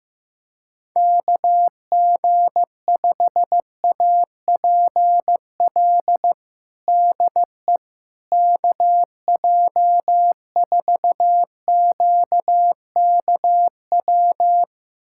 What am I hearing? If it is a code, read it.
Morse code, heard as a keyed tone, about 15 wpm, 700 Hz. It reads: KG5APL DE KJ4QKW
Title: USA Callsign DE Callsign 15wpm